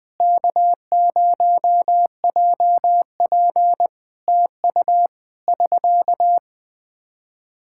Morse code, heard as a keyed tone, about 20 wpm, 700 Hz. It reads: K0JP TU <SK>
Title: Practice QSO - Exchange 4 - Ending - 20wpm